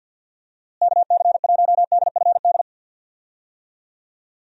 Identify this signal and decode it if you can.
Morse code, heard as a keyed tone, about 53 wpm, 700 Hz. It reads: KC1DFD
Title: USA Call Signs - 6 Characters - Repeated 2x - 50wpm